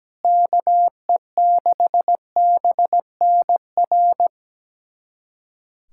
Morse code, heard as a keyed tone, about 17 wpm, 700 Hz. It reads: KE6BNR
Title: USA Callsign DE Callsign - 6 Characters - 17wpm